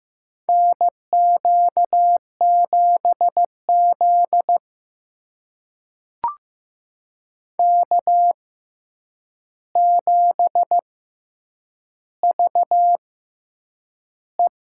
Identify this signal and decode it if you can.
Morse code, heard as a keyed tone, about 15 wpm, spaced as for 8 wpm, 700 Hz. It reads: NQ7Z K7VE
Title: USA Call Signs - 4 Characters - 6x Character Spacing - 15wpm